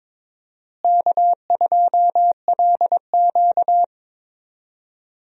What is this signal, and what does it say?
Morse code, heard as a keyed tone, about 22 wpm, 700 Hz. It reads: K2LQ
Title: USA Call Signs - 4 Characters - Repeated 3x - 22wpm